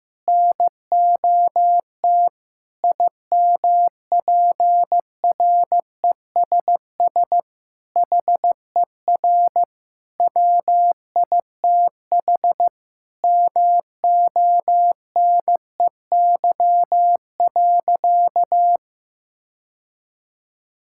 Morse code, heard as a keyed tone, about 15 wpm, 700 Hz. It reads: NOT IMPRESS HER WITH MONEY.
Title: Sentences from Top 1600 Words 15wpm